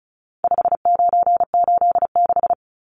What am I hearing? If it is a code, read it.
Morse code, heard as a keyed tone, about 34 wpm, 700 Hz. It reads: 5986
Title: Numbers - 4-Digit 35wpm